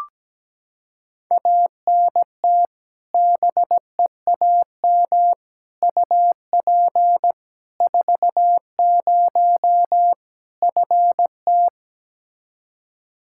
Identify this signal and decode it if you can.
Morse code, heard as a keyed tone, about 17 wpm, 700 Hz. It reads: ANT BEAM UP 40 FT
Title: Practice QSO - Exchange 2 - Info - 17wpm